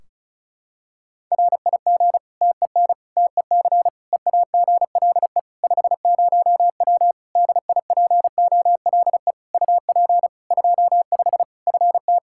Morse code, heard as a keyed tone, about 36 wpm, 700 Hz. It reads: RIG TEN TEC EAGLE 50W DIPOLE UP 25 FT
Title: Practice QSO - Exchange 2 - Info - 35wpm